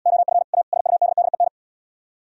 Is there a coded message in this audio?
Yes: Morse code from 0.1 to 1.5 s, heard as a keyed tone, about 77 wpm, 700 Hz. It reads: OF A SUDDEN